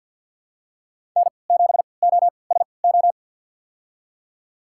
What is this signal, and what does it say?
Morse code, heard as a keyed tone, about 51 wpm, spaced as for 28 wpm, 700 Hz. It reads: N7OSO
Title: USA Call Signs - 5 Characters - 3x Character Spacing - 50wpm